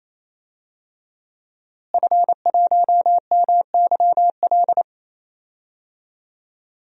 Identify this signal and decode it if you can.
Morse code, heard as a keyed tone, about 28 wpm, 700 Hz. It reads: F1MYL
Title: DXCC-100 Call Signs - 5 Characters - Repeated 3x - 28wpm